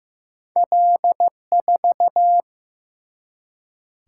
Morse code, heard as a keyed tone, about 15 wpm, 700 Hz. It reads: L4
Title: Single-Letter Single-Number 15wpm